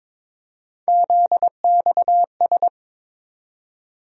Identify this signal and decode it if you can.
Morse code, heard as a keyed tone, about 22 wpm, 700 Hz. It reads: ZXS
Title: USA Call Sign Postfixes - 3 Letters 22wpm